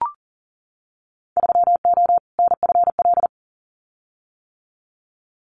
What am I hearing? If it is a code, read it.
Morse code, heard as a keyed tone, about 40 wpm, 700 Hz. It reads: 3O NFL
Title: Winter Field Day Exchanges 40wpm